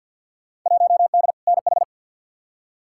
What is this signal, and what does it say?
Morse code, heard as a keyed tone, about 47 wpm, 700 Hz. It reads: 1D NH
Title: Field Day Exchanges - 50wpm